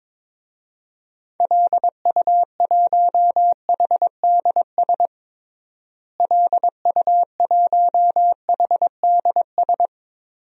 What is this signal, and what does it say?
Morse code, heard as a keyed tone, about 22 wpm, 700 Hz. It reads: LU1HDS LU1HDS
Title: DXCC-100 Call Signs - 6 Characters - Repeated 3x - 22wpm